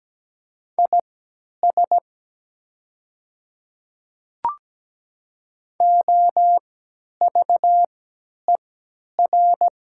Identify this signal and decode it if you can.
Morse code, heard as a keyed tone, about 17 wpm, spaced as for 10 wpm, 700 Hz. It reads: IS OVER
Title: 3x Character Spacing - Top 100 Words - 17wpm